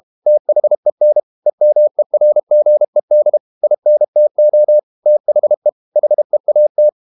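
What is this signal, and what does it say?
Morse code, heard as a keyed tone, about 32 wpm, 600 Hz. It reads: THEN EMERGED INTO THE HEAT
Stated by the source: Morse code practice at 32 wpm from CW med Gnister, reading Ken Follett's novel Never in English.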